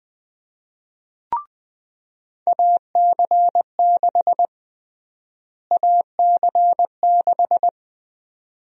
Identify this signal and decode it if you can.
Morse code, heard as a keyed tone, about 20 wpm, 700 Hz. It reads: AC6 AC6
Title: USA Call Sign Prefixes - Repeated 2x - 20wpm